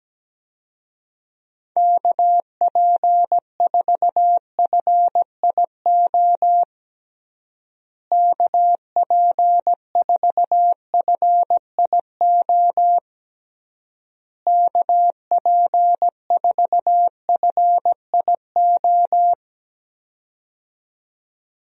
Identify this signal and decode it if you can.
Morse code, heard as a keyed tone, about 17 wpm, 700 Hz. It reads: KP4FIO KP4FIO KP4FIO
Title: USA Call Signs - 6 Characters - Repeated 3x - 17wpm